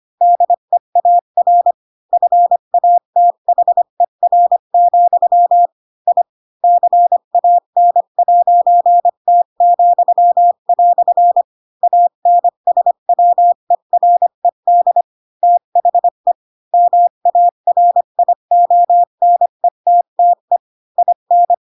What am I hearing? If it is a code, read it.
Morse code, heard as a keyed tone, about 25 wpm, 700 Hz. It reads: DEAR FATHER, I CAN'T," ANSWERED THE MARIONETTE IN
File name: Pinocchio / Pinocchio ch7 20 25 ik1hge